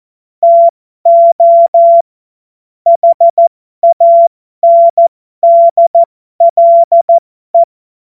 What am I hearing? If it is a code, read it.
Morse code, heard as a keyed tone, about 14 wpm, spaced as for 12 wpm, 680 Hz. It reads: TO HANDLE
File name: Morsekurs / Witze